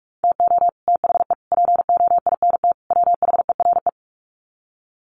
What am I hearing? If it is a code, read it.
Morse code, heard as a keyed tone, about 44 wpm, 700 Hz. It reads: TO THE POINT WHERE
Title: Sets of 4 Words 45wpm